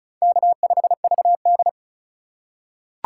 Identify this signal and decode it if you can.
Morse code, heard as a keyed tone, about 34 wpm, 700 Hz. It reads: K5VD
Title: USA Call Signs - 4 Characters - 6x Character Spacing - 35wpm